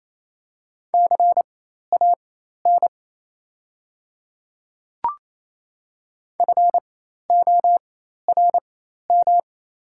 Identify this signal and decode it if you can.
Morse code, heard as a keyed tone, about 28 wpm, spaced as for 13 wpm, 700 Hz. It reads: CAN FORM
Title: 4x Character Spacing - Top 300 Words - 28wpm